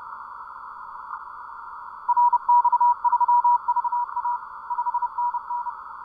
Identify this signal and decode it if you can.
Morse code, heard as a keyed tone, about 30 wpm, 970 Hz. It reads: RX3VA 5NN